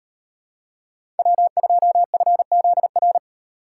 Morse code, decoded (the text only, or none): W2FZR